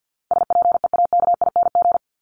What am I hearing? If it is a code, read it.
Morse code, heard as a keyed tone, about 49 wpm, 700 Hz. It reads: SPEAKING
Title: ING Suffix - Top 100 Words with Base Words 50wpm